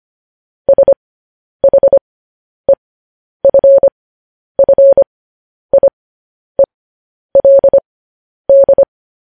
Morse code, heard as a keyed tone, about 25 wpm, spaced as for 10 wpm, 550 Hz. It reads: SHEFFIELD
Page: news headlines in morse code at 10 wpm